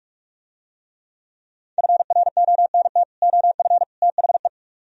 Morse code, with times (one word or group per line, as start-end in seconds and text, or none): 1.8-3.0 FRONT
3.2-3.8 OF
4.0-4.5 THE